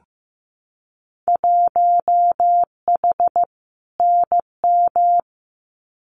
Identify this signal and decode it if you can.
Morse code, heard as a keyed tone, about 15 wpm, 700 Hz. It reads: 1H NM
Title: Winter Field Day Exchanges 15wpm